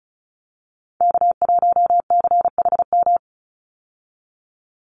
Morse code, heard as a keyed tone, about 34 wpm, 700 Hz. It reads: K1CHM
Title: Call Signs Repeated 2x - 35wpm